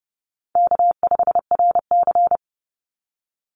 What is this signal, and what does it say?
Morse code, heard as a keyed tone, about 30 wpm, 700 Hz. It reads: K5RC
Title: Call Signs Repeated 3x - 30wpm